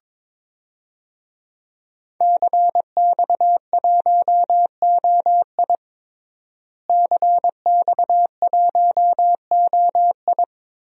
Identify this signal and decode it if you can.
Morse code, heard as a keyed tone, about 22 wpm, 700 Hz. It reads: CX1OI CX1OI
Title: DXCC-100 Call Signs - 5 Characters - Repeated 2x - 22wpm